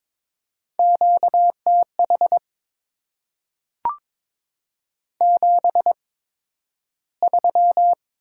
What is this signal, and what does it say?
Morse code, heard as a keyed tone, about 22 wpm, spaced as for 12 wpm, 700 Hz. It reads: QTH 73
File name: Top 20 QSO Elements - 8x Character Spacing - 22wpm